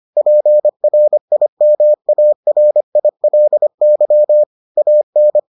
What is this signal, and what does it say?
Morse code, heard as a keyed tone, about 25 wpm, 600 Hz. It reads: PRIMARILY AN